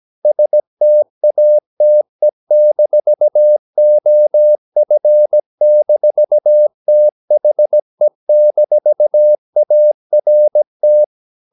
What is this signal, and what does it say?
Morse code, heard as a keyed tone, about 17 wpm, 600 Hz. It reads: STATE-OF-THE-ART
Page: Romanen Never af Ken Follet på engelsk.